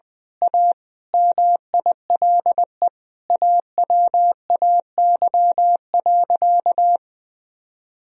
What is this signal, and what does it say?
Morse code, heard as a keyed tone, about 20 wpm, 700 Hz. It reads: A MILE AWAY.
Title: Sentences from Top 1000 Words - Review Part 2 - 20wpm